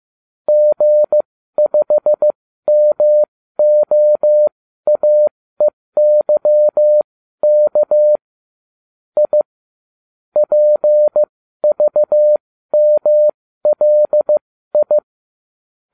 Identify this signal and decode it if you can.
Morse code, heard as a keyed tone, about 15 wpm, spaced as for 12 wpm, 600 Hz. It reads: G5MOAEYK I PVMLI